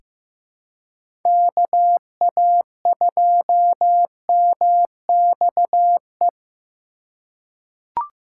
Morse code, heard as a keyed tone, about 15 wpm, 700 Hz. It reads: KA2MXE
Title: USA Call Signs - 6 Characters - 4x Character Spacing - 15wpm